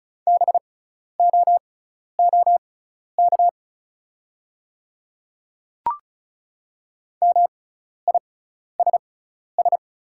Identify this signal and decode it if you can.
Morse code, heard as a keyed tone, about 36 wpm, spaced as for 12 wpm, 700 Hz. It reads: BOOK MISS
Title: 6x Character Spacing - Top 500 Words - 35wpm